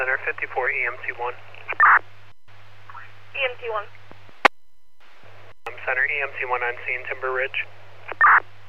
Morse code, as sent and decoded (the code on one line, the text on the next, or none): none